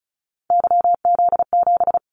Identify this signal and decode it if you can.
Morse code, heard as a keyed tone, about 34 wpm, 700 Hz. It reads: YZ7
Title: Callsign Prefixes for Top 25 DXCC - 35wpm